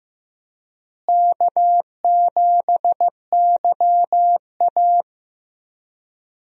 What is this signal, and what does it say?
Morse code, heard as a keyed tone, about 15 wpm, 700 Hz. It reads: K7YA
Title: USA Call Signs - 4 Characters - Repeated 2x - 15wpm